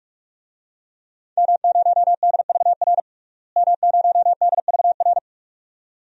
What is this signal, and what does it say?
Morse code, heard as a keyed tone, about 43 wpm, 700 Hz. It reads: M0DVR M0DVR
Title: DXCC-100 Call Signs - 5 Characters - Repeated 2x - 45wpm